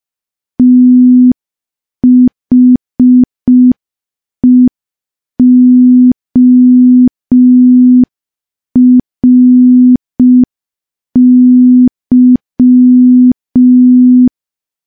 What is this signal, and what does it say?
Morse code, heard as a keyed tone, about 5 wpm, 250 Hz. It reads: THEORY
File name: morse / morse-training-05WPM